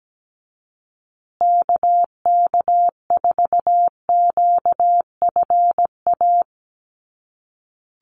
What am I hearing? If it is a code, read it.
Morse code, heard as a keyed tone, about 17 wpm, 700 Hz. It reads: KK4QFA
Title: USA Call Signs Repeated 3x 17wpm